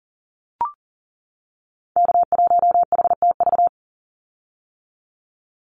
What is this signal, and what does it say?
Morse code, heard as a keyed tone, about 40 wpm, 700 Hz. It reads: K1HTV
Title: Call Signs - 40wpm